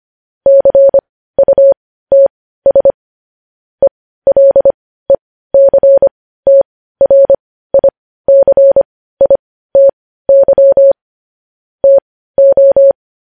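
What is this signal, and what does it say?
Morse code, heard as a keyed tone, about 25 wpm, spaced as for 15 wpm, 550 Hz. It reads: CUTS ELECTRICITY TO
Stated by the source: Morse code sending of news headlines at 15 wpm.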